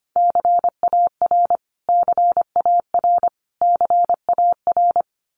Morse code, heard as a keyed tone, about 25 wpm, 700 Hz. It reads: CAR CAR CAR
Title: Top 500 Words Repeated 25wpm